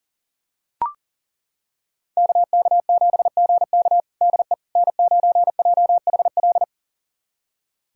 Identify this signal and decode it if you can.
Morse code, heard as a keyed tone, about 40 wpm, 700 Hz. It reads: KK7GK DE N9JHL
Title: USA Callsign DE Callsign - 5 Characters - 40wpm